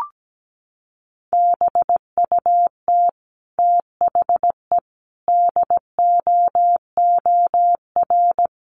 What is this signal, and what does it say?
Morse code, heard as a keyed tone, about 17 wpm, 700 Hz. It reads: BUT THE DOOR
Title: Sentences from Top 600 Words 17wpm